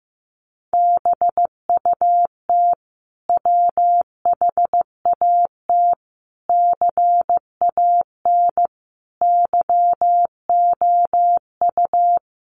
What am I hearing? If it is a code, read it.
Morse code, heard as a keyed tone, about 15 wpm, 700 Hz. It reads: BUT WHAT CAN YOU
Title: Sentences from Top 1000 Words 15wpm